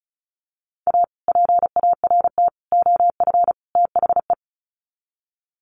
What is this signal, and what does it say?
Morse code, heard as a keyed tone, about 34 wpm, 700 Hz. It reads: A PART OF THE
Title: Sets of 4 Words 35wpm